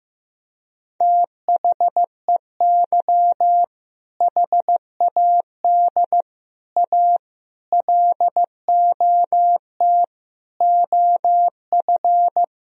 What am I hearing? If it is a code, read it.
Morse code, heard as a keyed tone, about 15 wpm, 700 Hz. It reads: THEY HAD A LOT OF
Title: Sentences from Top 1000 Words - Review Part 3 - 15wpm